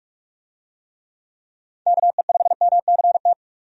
Morse code, heard as a keyed tone, about 45 wpm, 700 Hz. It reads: KE5MCT
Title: USA Call Signs - 6 Characters - 45wpm